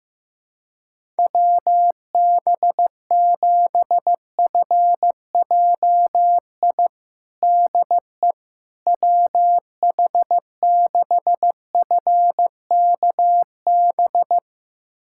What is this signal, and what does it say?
Morse code, heard as a keyed tone, about 15 wpm, 700 Hz. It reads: WB7FJI DE WH6FKB
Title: USA Callsign DE Callsign - 6 Characters - 15wpm